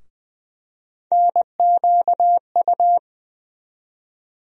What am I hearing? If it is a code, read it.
Morse code, heard as a keyed tone, about 20 wpm, 700 Hz. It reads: NQU
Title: USA Call Sign Postfixes - 3 Letters 20wpm